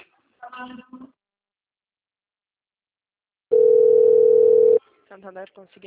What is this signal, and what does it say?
Ringback tone (North America), 440 + 480 Hz, with speech.
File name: rg-1600-unknown- - - .21831